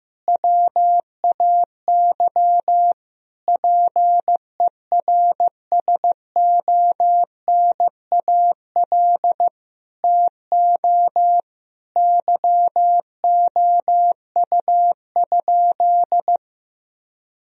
Morse code, heard as a keyed tone, about 15 wpm, 700 Hz. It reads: WAY PERSONAL TO YOU?
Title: Sentences from Top 1000 Words - Review Part 4 - 15wpm